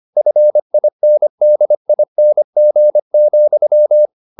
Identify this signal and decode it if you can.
Morse code, heard as a keyed tone, about 25 wpm, 600 Hz. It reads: FINDING,
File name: the old man and the sea part 1 25wpm